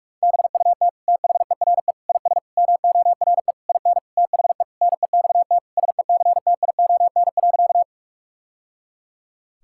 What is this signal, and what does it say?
Morse code, heard as a keyed tone, about 46 wpm, 700 Hz. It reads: BUT THERE IS MORE IN THE NEXT SECTION.
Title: Sentences from Top 1000 Words - Review Part 5 - 45wpm